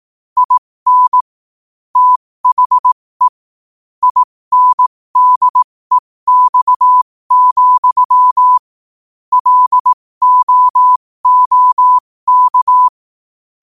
Morse code, heard as a keyed tone, about 18 wpm, spaced as for 15 wpm, 1000 Hz. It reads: IN THE INDEX, LOOK
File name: qotdfiles / 15wpm